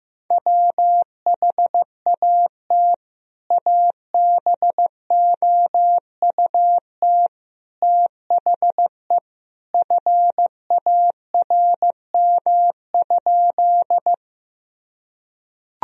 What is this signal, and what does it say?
Morse code, heard as a keyed tone, about 15 wpm, 700 Hz. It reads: WHAT ABOUT THE FARM?
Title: Sentences from Top 1000 Words - Review Part 2 - 15wpm